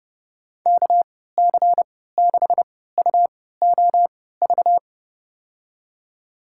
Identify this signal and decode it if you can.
Morse code, heard as a keyed tone, about 30 wpm, spaced as for 17 wpm, 700 Hz. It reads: KC6UOV
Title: USA Call Signs - 3x Character Spacing - 30wpm